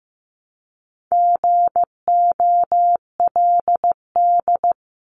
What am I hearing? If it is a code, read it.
Morse code, heard as a keyed tone, about 15 wpm, 700 Hz. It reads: GOLD
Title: Top 500 Words 15wpm